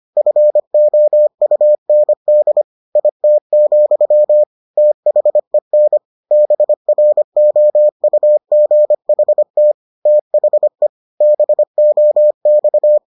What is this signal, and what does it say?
Morse code, heard as a keyed tone, about 25 wpm, 600 Hz. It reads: FOUND IT, THEN BROUGHT THE BOX